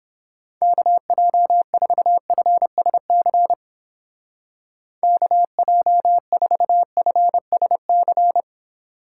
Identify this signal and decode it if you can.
Morse code, heard as a keyed tone, about 31 wpm, 700 Hz. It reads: KJ4FSC KJ4FSC